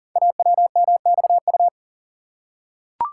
Morse code, heard as a keyed tone, about 42 wpm, 690 Hz. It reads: AWMXU
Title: Code Groups - 5 Letters - 40wpm